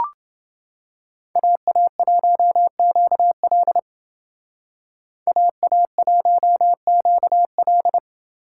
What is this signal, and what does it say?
Morse code, heard as a keyed tone, about 29 wpm, 700 Hz. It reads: AA1QL AA1QL